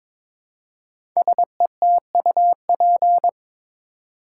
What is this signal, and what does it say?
Morse code, heard as a keyed tone, about 22 wpm, 700 Hz. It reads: SETUP
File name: 5 Letter Words - First 3 Letters Given - 22wpm